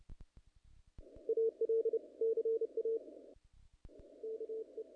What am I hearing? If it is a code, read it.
Morse code, heard as a keyed tone, about 28 wpm, 440 Hz. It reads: AL CA KE
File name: contests / naqp-cw-k5tr-op-k5pi- -1800